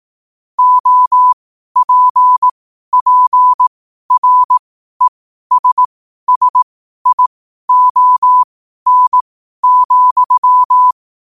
Morse code, heard as a keyed tone, about 18 wpm, spaced as for 12 wpm, 1000 Hz. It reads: OPPRESSION,
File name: qotdfiles / 12wpm